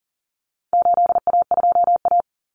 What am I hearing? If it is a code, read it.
Morse code, heard as a keyed tone, about 40 wpm, 700 Hz. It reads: 8A2A
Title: Common Cut Numbers - 4-Digit 40wpm